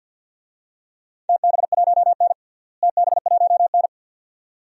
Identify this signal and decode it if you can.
Morse code, heard as a keyed tone, about 50 wpm, 700 Hz. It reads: TB1N TB1N
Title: DXCC-100 Call Signs - 4 Characters - Repeated 2x - 50wpm